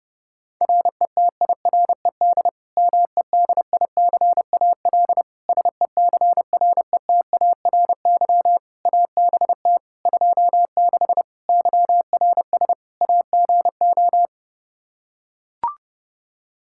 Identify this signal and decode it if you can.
Morse code, heard as a keyed tone, about 32 wpm, 700 Hz. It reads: RETIRED MEDICAL SECRETARY ABT 26 YRS AGO
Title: K1USN SST QSO Practice 30wpm